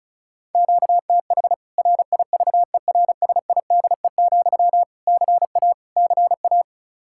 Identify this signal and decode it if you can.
Morse code, heard as a keyed tone, about 34 wpm, 700 Hz. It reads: QTH RIVERSIDE, CA CA
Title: K1USN SST QSO Practice 35wpm